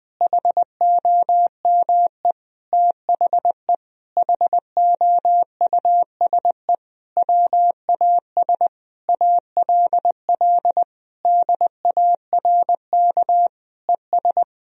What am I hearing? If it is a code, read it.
Morse code, heard as a keyed tone, about 21 wpm, 700 Hz. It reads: HOME THE HOUSE WAS ALL DARK ES
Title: Sentences from Top 500 Words and Abbreviations 20wpm